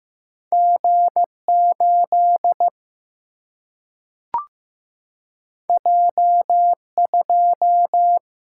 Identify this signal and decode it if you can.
Morse code, heard as a keyed tone, about 15 wpm, 700 Hz. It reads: G 8 J 2
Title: Single-Letter Single-Number 15wpm